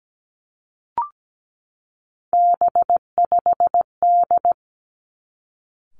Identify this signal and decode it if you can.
Morse code, heard as a keyed tone, about 17 wpm, 700 Hz. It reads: B5D
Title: Three Challenging Letter-Number Combinations 17wpm